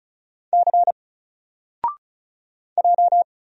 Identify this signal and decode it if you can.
Morse code, heard as a keyed tone, about 35 wpm, 700 Hz. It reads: C J
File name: Single-Letter-Number-35wpm